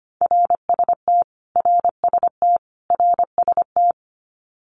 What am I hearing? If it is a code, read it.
Morse code, heard as a keyed tone, about 25 wpm, 700 Hz. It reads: RST RST RST
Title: Abbreviations Repeated 3x - 25wpm